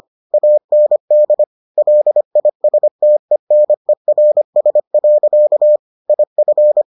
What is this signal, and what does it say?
Morse code, heard as a keyed tone, about 25 wpm, 600 Hz. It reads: AND LISTENERS. IF